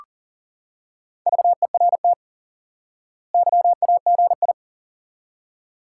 Morse code, heard as a keyed tone, about 37 wpm, 700 Hz. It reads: VERT YAGI
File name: IQE Course 3 - Lesson 89 - YAGI - Two Elements - 40wpm